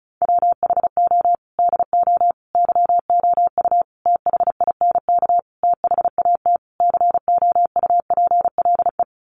Morse code, heard as a keyed tone, about 34 wpm, 700 Hz. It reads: WHO DO YOU THINK THAT COUPLE
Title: Sentences from Top 1100 Words 35wpm